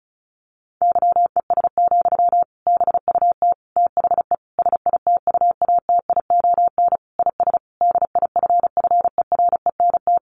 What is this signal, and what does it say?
Morse code, heard as a keyed tone, about 34 wpm, 700 Hz. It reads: YES, BUT THE SITUATION IS DIFFERENT
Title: Sentences from Top 1200 Words 35wpm